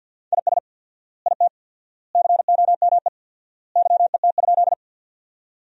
Morse code, heard as a keyed tone, about 50 wpm, 700 Hz. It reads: IS IT COME YET?
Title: 4x Word-Spacing - Sentences from Top 500 Words - 50wpm